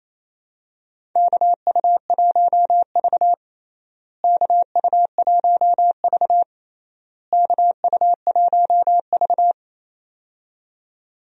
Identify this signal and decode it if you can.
Morse code, heard as a keyed tone, about 28 wpm, 700 Hz. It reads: KU1V KU1V KU1V
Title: USA Call Signs - 4 Characters - Repeated 3x - 28wpm